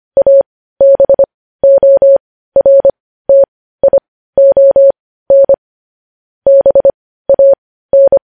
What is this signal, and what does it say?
Morse code, heard as a keyed tone, about 25 wpm, spaced as for 15 wpm, 550 Hz. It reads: ABORTION BAN
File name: podcast / News headlines-1x-25x15